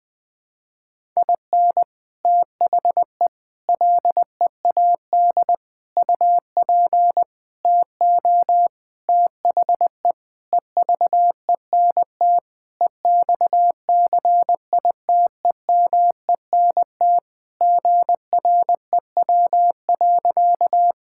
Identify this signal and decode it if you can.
Morse code, heard as a keyed tone, about 20 wpm, 700 Hz. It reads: IN THE LEAD UP TO THE EVENT EXCITEMENT GREW.
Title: Sentences with 6-Word Phrases - 20wpm